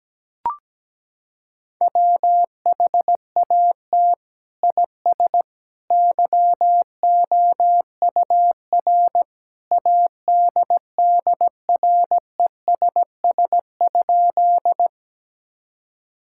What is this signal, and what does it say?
Morse code, heard as a keyed tone, about 17 wpm, 700 Hz. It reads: WHAT IS YOUR ADDRESS?
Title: Sentences from Top 1000 Words - Review Part 6 - 17wpm